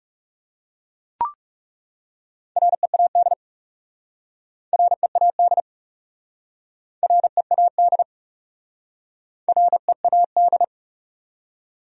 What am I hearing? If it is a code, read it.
Morse code, heard as a keyed tone, about 40 wpm, 700 Hz. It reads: READ READ READ READ